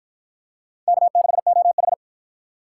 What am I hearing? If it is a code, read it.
Morse code, heard as a keyed tone, about 47 wpm, 700 Hz. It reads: K6OH